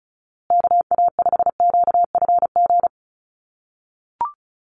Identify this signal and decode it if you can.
Morse code, heard as a keyed tone, about 34 wpm, 700 Hz. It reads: KA5QFG
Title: USA Call Signs Repeated 3x 35wpm